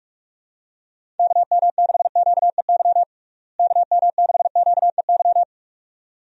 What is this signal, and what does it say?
Morse code, heard as a keyed tone, about 41 wpm, 700 Hz. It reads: KM6QEY KM6QEY
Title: USA Call Signs - 6 Characters - Repeated 2x - 45wpm